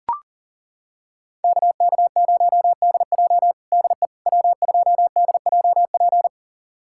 Morse code, heard as a keyed tone, about 40 wpm, 700 Hz. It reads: KK0DJ DE W2DJP
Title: USA Callsign DE Callsign - 5 Characters - 40wpm